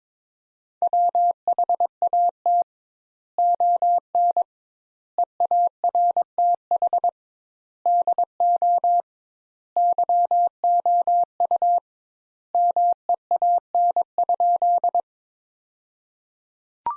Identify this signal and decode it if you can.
Morse code, heard as a keyed tone, about 22 wpm, 700 Hz. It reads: WHAT ON EARTH DO YOU MEAN?
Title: Sentences Top 200 Words - 22wpm